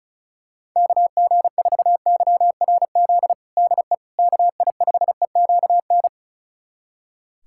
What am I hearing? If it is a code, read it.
Morse code, heard as a keyed tone, about 34 wpm, 690 Hz. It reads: KG4YRZ DE KI5EQN
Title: USA Callsign DE Callsign - 6 Characters - 35wpm